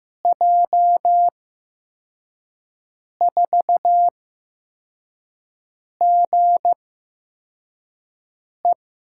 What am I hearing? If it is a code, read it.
Morse code, heard as a keyed tone, about 15 wpm, spaced as for 4 wpm, 700 Hz. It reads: J4GE
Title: USA Call Signs - 5 Characters - 8x Character Spacing - 15wpm